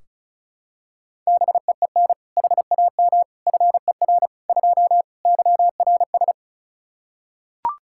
Morse code, heard as a keyed tone, about 36 wpm, 700 Hz. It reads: BEEN HAM FER 2 YRS
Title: Practice QSO - Exchange 3 - More Info - 35wpm